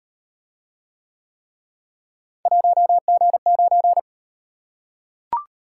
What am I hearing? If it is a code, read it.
Morse code, heard as a keyed tone, about 37 wpm, 700 Hz. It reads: 1G9